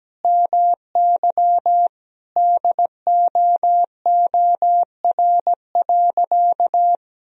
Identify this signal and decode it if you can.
Morse code, heard as a keyed tone, about 17 wpm, 700 Hz. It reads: MY DOOR.
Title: Sentences from Top 1000 Words - Review Part 1 - 17wpm